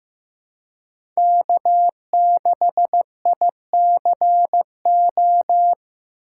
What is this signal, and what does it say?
Morse code, heard as a keyed tone, about 15 wpm, 700 Hz. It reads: K6ICO
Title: USA Call Signs - 5 Characters - 3x Character Spacing - 15wpm